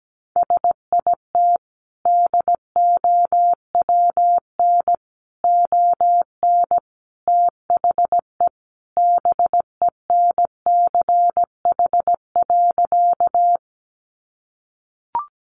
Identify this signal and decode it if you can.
Morse code, heard as a keyed tone, about 17 wpm, 700 Hz. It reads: SIT DOWN ON THE BENCH.
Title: Sentences from Top 1400 Words 17wpm